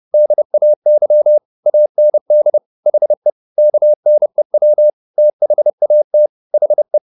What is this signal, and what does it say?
Morse code, heard as a keyed tone, about 30 wpm, 600 Hz. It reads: DAY AND HE KNEW THAT HE
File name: the old man and the sea / the old man and the sea 1 30wpm